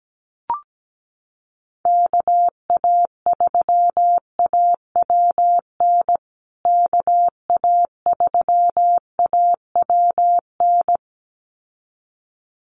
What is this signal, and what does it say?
Morse code, heard as a keyed tone, about 17 wpm, 700 Hz. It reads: KA3AWN KA3AWN
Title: USA Call Signs Repeated 2x 17wpm